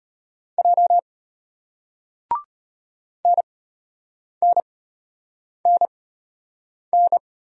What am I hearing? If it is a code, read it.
Morse code, heard as a keyed tone, about 34 wpm, spaced as for 8 wpm, 700 Hz. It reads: J NNNN